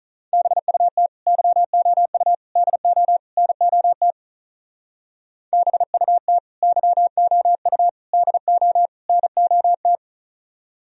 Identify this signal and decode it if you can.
Morse code, heard as a keyed tone, about 44 wpm, 700 Hz. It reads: BUT YOU DO NOT BUT YOU DO NOT